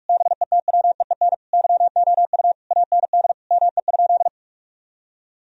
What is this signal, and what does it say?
Morse code, heard as a keyed tone, about 41 wpm, 700 Hz. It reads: BETWEEN YOU AND ME?
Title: Sentences from Top 1000 Words - Review Part 2 - 45wpm